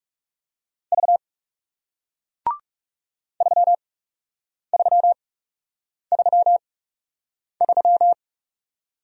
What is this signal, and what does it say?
Morse code, heard as a keyed tone, about 40 wpm, spaced as for 8 wpm, 700 Hz. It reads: V 3333